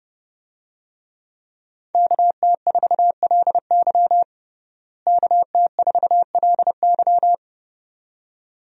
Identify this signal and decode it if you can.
Morse code, heard as a keyed tone, about 30 wpm, 700 Hz. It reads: KT4LY KT4LY
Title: USA Call Signs - 5 Characters - Repeated 2x - 30wpm